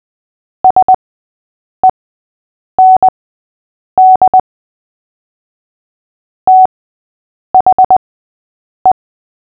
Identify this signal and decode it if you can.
Morse code, heard as a keyed tone, about 20 wpm, spaced as for 8 wpm, 750 Hz. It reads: SEND THE
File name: cw / fists / Short-Story-6-8wpm